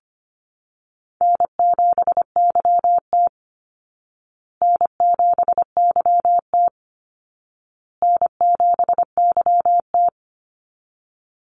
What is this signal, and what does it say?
Morse code, heard as a keyed tone, about 25 wpm, 700 Hz. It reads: N7YT N7YT N7YT
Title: Call Signs Repeated 2x - 25wpm